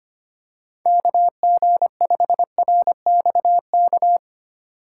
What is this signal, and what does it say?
Morse code, heard as a keyed tone, about 25 wpm, 700 Hz. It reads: KG5RXK
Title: USA Call Signs - 3x Character Spacing - 25wpm